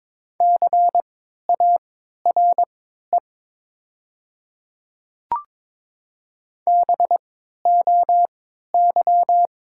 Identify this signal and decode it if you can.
Morse code, heard as a keyed tone, about 23 wpm, spaced as for 13 wpm, 700 Hz. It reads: CARE BOY
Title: 3x Character Spacing - Top 500 Words - 22wpm